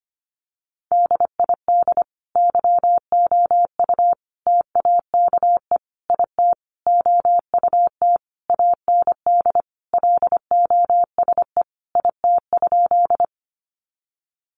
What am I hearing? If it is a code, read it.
Morse code, heard as a keyed tone, about 25 wpm, 700 Hz. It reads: DID YOU TAKE IT OUT AND LOSE IT?
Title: Sentences from Top 1100 Words 25wpm